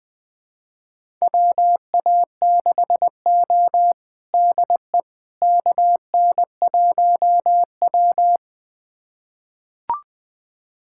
Morse code, heard as a keyed tone, about 20 wpm, 700 Hz. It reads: WA6O DE KN1W
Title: USA Callsign DE Callsign - 4 Characters - 20wpm